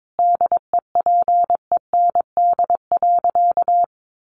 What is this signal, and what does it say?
Morse code, heard as a keyed tone, about 22 wpm, 700 Hz. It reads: DEPEND.
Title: Sentences from Top 1300 Words 22wpm